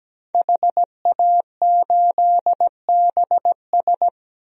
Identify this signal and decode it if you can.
Morse code, heard as a keyed tone, about 17 wpm, 700 Hz. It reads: HA8BS
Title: DXCC-100 Call Signs - 5 Characters - Repeated 2x - 17wpm